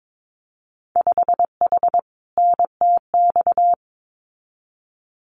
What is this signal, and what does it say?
Morse code, heard as a keyed tone, about 22 wpm, 700 Hz. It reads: 5H NTX
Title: Winter Field Day Exchanges Encore 22wpm